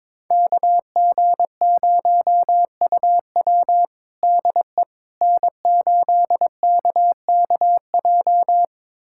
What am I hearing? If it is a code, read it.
Morse code, heard as a keyed tone, about 22 wpm, 700 Hz. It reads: KG0UW DE N8KKJ
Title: USA Callsign DE Callsign - 5 Characters - 22wpm